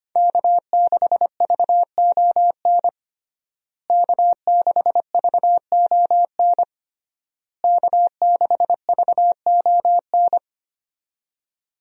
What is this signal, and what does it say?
Morse code, heard as a keyed tone, about 25 wpm, 700 Hz. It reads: K6VON K6VON K6VON
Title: USA Call Signs - 5 Characters - Repeated 3x - 25wpm